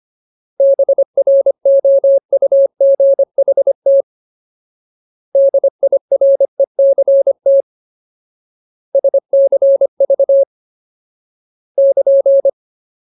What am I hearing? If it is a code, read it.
Morse code, heard as a keyed tone, about 25 wpm, 550 Hz. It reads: BROUGHT DIRECT SCV <KN>
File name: Ultimate Word List / 25WPM Ultimate